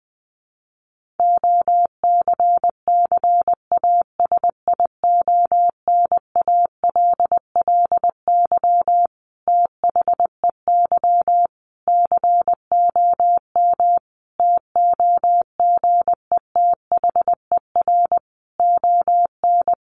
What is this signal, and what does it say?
Morse code, heard as a keyed tone, about 20 wpm, 700 Hz. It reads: OCCASIONALLY THEY COM TOGETHER ON
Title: Sentences from Top 1300 Words 20wpm